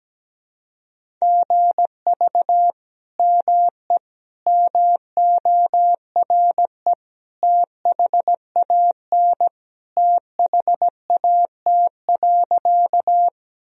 Morse code, heard as a keyed tone, about 17 wpm, 700 Hz. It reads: GV ME MORE THAN THAT.
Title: Sentences from Top 200 Words and Abbreviations 17wpm